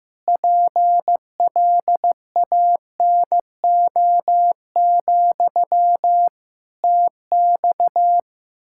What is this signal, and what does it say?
Morse code, heard as a keyed tone, about 15 wpm, 700 Hz. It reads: PLANO, TX
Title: Practice QSO - Exchange 1 - Beginning - 15wpm